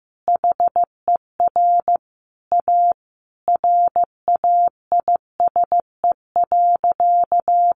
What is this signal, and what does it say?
Morse code, heard as a keyed tone, about 15 wpm, 700 Hz. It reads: HER A RAISE.
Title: Easy Phrases 15wpm